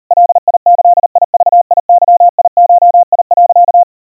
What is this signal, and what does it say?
Morse code (Japanese Wabun code), heard as a keyed tone, about 40 wpm, 700 Hz. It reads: ガギグゲゴ、
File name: iroha 130 cpm 700 Hz Crispy